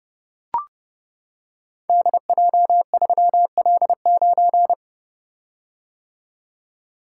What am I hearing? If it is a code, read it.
Morse code, heard as a keyed tone, about 29 wpm, 700 Hz. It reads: DJ3L9
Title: Code Groups - 5 Letter:Numbers - 30wpm